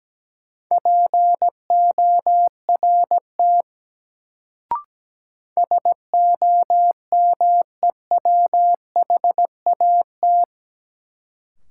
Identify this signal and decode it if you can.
Morse code, heard as a keyed tone, about 17 wpm, 700 Hz. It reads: PORT SOMEWHAT
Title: Sentences from Top 1000 Words - Review Part 1 - New Word Drill - 17wpm